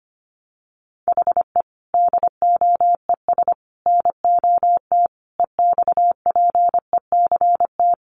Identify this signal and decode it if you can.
Morse code, heard as a keyed tone, about 25 wpm, 700 Hz. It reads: HE DOES NOT EXPECT
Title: Sentences from Top 1100 Words 25wpm